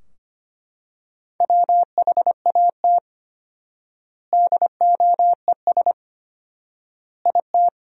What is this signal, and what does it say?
Morse code, heard as a keyed tone, about 25 wpm, 700 Hz. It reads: WHAT DOES IT
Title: 4x Word-Spacing - Sentences from Top 200 Words - 25wpm